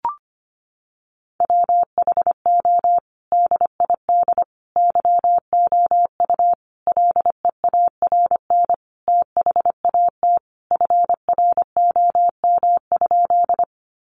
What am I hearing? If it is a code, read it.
Morse code, heard as a keyed tone, about 25 wpm, 700 Hz. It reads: WHO DID YOU LEARN THAT FROM?
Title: Sentences from Top 300 Words 25wpm